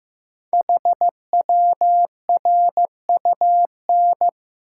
Morse code, heard as a keyed tone, about 15 wpm, 700 Hz. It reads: HWRUN
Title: Code Groups - 5 Letters - 15wpm